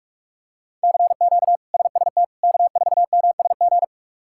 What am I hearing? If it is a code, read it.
Morse code, heard as a keyed tone, about 44 wpm, 700 Hz. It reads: CQ SST K4MSG
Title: K1USN SST QSO Practice - 45wpm